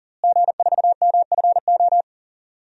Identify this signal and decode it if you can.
Morse code, heard as a keyed tone, about 40 wpm, 690 Hz. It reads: G4MFO
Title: DXCC-100 Call Signs - 5 Characters - Repeated 2x - 40wpm